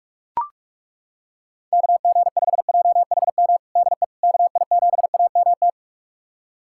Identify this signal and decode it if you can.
Morse code, heard as a keyed tone, about 41 wpm, 700 Hz. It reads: KG5JHM DE KI7AMT
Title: USA Callsign DE Callsign - 6 Characters - 45wpm